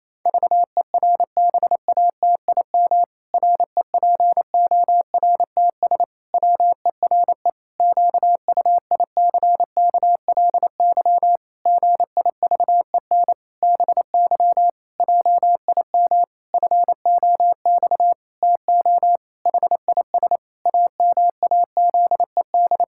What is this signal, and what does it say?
Morse code, heard as a keyed tone, about 28 wpm, 700 Hz. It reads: VERBATIM REPORTS WERE QUICKLY GIVEN BY JIM FOX TO HIS AMAZED
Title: Pangram Challenge v2 - 28wpm